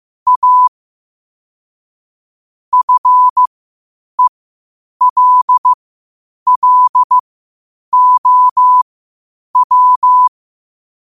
Morse code, heard as a keyed tone, about 15 wpm, spaced as for 8 wpm, 1000 Hz. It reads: A FELLOW